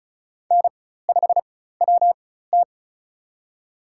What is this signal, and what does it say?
Morse code, heard as a keyed tone, about 34 wpm, spaced as for 16 wpm, 700 Hz. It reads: N5WT
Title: USA Call Signs - 4 Characters - 4x Character Spacing - 35wpm